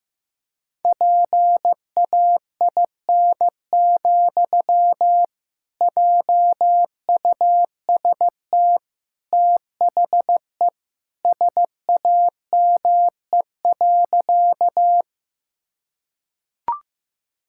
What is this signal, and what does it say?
Morse code, heard as a keyed tone, about 15 wpm, 700 Hz. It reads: PAIN, JUST THE SAME.
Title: Sentences from Top 1600 Words 15wpm